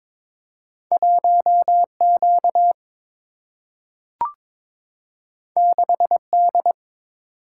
Morse code, heard as a keyed tone, about 22 wpm, 700 Hz. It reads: 1 Q 6 D
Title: Single-Letter Single-Number 22wpm